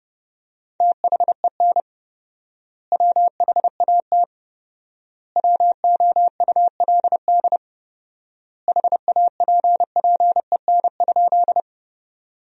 Morse code, heard as a keyed tone, about 30 wpm, 700 Hz. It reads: THEN WHAT WOULD HAPPEN?
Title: 4x Word-Spacing - Sentences from Top 400 Words - 30wpm